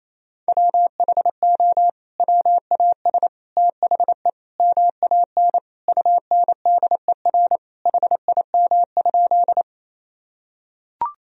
Morse code, heard as a keyed tone, about 28 wpm, 700 Hz. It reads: WHO WAS THE MAN UNDER HIM?
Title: Sentences from Top 1000 Words - Review Part 1 - 28wpm